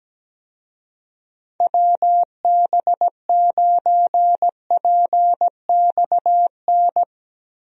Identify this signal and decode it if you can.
Morse code, heard as a keyed tone, about 17 wpm, 700 Hz. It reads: WB9PXN
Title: USA Call Signs - 6 Characters - Repeated 3x - 17wpm